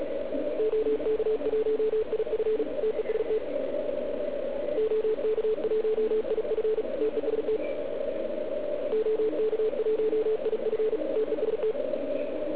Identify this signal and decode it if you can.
Morse code, heard as a keyed tone, about 36 wpm, 440 Hz, with radio noise. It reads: OK1IF TEST OK1IF TEST OK1IF TEST